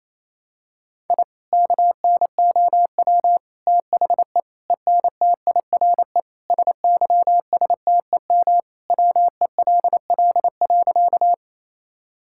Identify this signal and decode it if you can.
Morse code, heard as a keyed tone, about 28 wpm, 700 Hz. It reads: I KNOW THE ENTIRE SYSTEM WELL.
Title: Sentences from Top 1000 Words - Review Part 5 - 28wpm